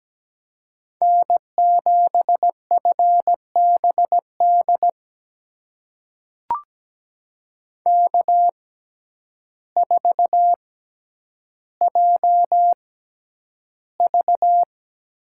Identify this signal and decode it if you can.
Morse code, heard as a keyed tone, about 17 wpm, spaced as for 9 wpm, 700 Hz. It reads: N7FBD K4JV
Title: USA Call Signs - 6x Character Spacing - 17wpm